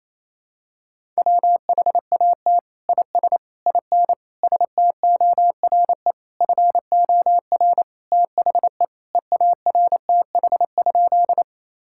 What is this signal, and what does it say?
Morse code, heard as a keyed tone, about 27 wpm, 700 Hz. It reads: WHAT IS IN STORE FOR THE EARTH?
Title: Sentences from Top 1000 Words - Review Part 5 - 28wpm